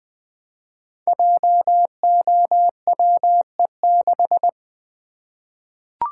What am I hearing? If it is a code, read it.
Morse code, heard as a keyed tone, about 20 wpm, 700 Hz. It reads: JOWE6
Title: Code Groups - 5 Letter:Numbers - 20wpm